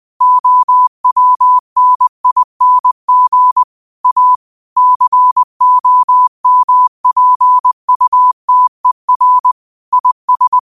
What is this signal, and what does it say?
Morse code, heard as a keyed tone, about 20 wpm, 1000 Hz. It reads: OWNING A COMPUTER IS
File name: qotdfiles / 20wpm